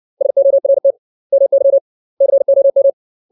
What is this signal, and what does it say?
Morse code, heard as a keyed tone, about 60 wpm, 550 Hz. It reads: SORT MY COM
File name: 632 words / 632 Words 60WPM